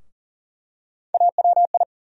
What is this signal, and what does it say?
Morse code, heard as a keyed tone, about 40 wpm, 700 Hz. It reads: AWI
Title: USA Call Sign Postfixes - 3 Letters 40wpm